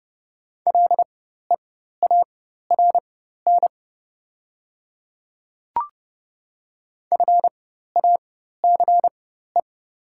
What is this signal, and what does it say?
Morse code, heard as a keyed tone, about 30 wpm, spaced as for 14 wpm, 700 Hz. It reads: LEARN FACE
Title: 4x Character Spacing - Top 300 Words - 30wpm